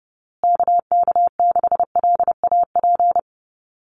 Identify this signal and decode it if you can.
Morse code, heard as a keyed tone, about 30 wpm, 700 Hz. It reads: KK6LAP
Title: USA Call Signs Repeated 2x 30wpm